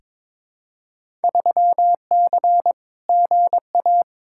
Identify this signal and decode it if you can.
Morse code, heard as a keyed tone, about 22 wpm, 700 Hz. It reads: 3C GA
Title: Field Day Exchanges Encore - 22wpm